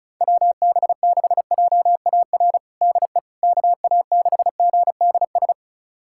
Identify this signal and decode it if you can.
Morse code, heard as a keyed tone, about 36 wpm, 700 Hz. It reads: WB6JAR DE KA6GDS
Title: USA Callsign DE Callsign - 6 Characters - 35wpm